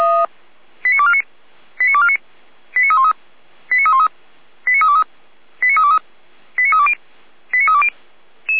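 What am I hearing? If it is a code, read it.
Five-tone selective call (ZVEI), heard as quick runs of five tones, tones of about 70 ms. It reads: 89249 89249 89313 89313 89323 89323 89320 89320
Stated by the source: ZVEI five-tone call 89310, received by SDR.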